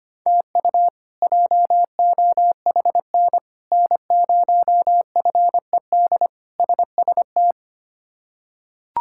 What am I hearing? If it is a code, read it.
Morse code, heard as a keyed tone, about 25 wpm, 700 Hz. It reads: TU JOHN N0FED SST
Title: K1USN SST QSO Practice - 25wpm